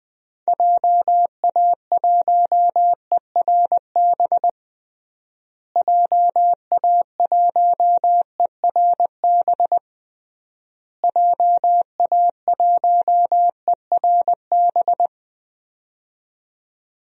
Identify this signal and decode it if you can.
Morse code, heard as a keyed tone, about 20 wpm, 700 Hz. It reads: JA1ERB JA1ERB JA1ERB
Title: DXCC-100 Call Signs - 6 Characters - Repeated 3x - 20wpm